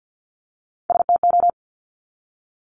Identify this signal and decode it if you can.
Morse code, heard as a keyed tone, about 50 wpm, 700 Hz. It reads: STO
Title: Top Three Letter Combinations in Words 50wpm